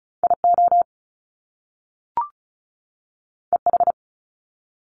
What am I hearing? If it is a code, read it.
Morse code, heard as a keyed tone, about 36 wpm, 700 Hz. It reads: IO EH
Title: Top Two Letter Combinations in Words 35wpm